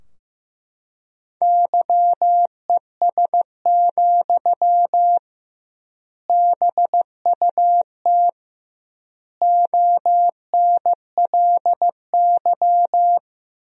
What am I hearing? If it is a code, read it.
Morse code, heard as a keyed tone, about 15 wpm, 700 Hz. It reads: YES, BUT ONLY
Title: 2x Word-Spacing - Sentences from Top 500 Words - 15wpm